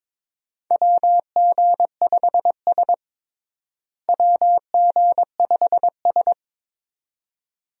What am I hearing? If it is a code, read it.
Morse code, heard as a keyed tone, about 22 wpm, 700 Hz. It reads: WG5S WG5S
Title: USA Call Signs - 4 Characters - Repeated 2x - 22wpm